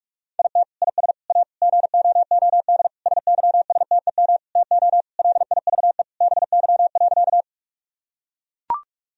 Morse code, heard as a keyed tone, about 41 wpm, 700 Hz. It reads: IT IS A GOOD SYSTEM TO LIVE BY.
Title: Sentences from Top 1000 Words - Review Part 3 - 45wpm